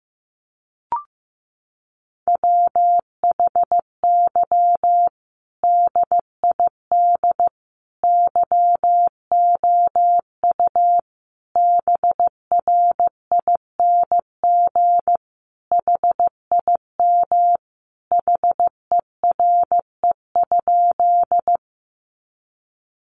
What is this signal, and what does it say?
Morse code, heard as a keyed tone, about 15 wpm, 700 Hz. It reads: WHY DID YOU BRING HIM HERE?
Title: Sentences from Top 500 Words 15wpm